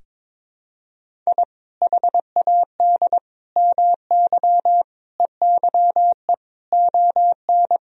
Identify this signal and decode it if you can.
Morse code, heard as a keyed tone, about 22 wpm, 700 Hz. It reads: I HAD MY EYE ON
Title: Sentences from Top 1000 Words - Review Part 1 - 22wpm